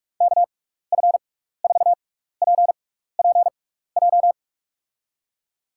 Morse code, heard as a keyed tone, about 44 wpm, spaced as for 15 wpm, 700 Hz. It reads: KF4PPJ
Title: USA Call Signs - 6 Characters - 6x Character Spacing - 45wpm